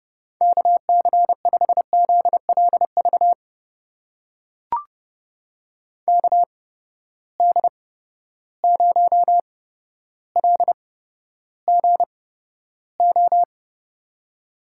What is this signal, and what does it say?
Morse code, heard as a keyed tone, about 32 wpm, spaced as for 13 wpm, 700 Hz. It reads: KC5ZLV KD0LGO
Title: USA Call Signs - 6 Characters - 8x Character Spacing - 30wpm